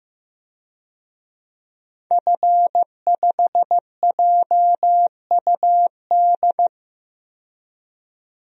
Morse code, heard as a keyed tone, about 15 wpm, 700 Hz. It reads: F5JUD
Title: DXCC-100 Call Signs - 5 Characters - 15wpm